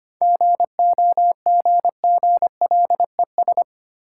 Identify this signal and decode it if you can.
Morse code, heard as a keyed tone, about 25 wpm, 690 Hz. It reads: GOGGLES
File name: practice / 25wpm / Easy Words with Hints - 25wpm